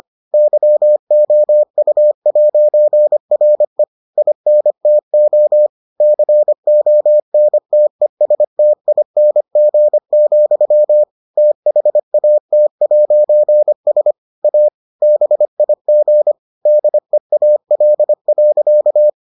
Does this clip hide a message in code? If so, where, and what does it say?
Yes: Morse code from 0.3 to 19.1 s, heard as a keyed tone, about 25 wpm, 600 Hz. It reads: YOU'RE INTO CONTESTING, THAT'S A BIG DEAL.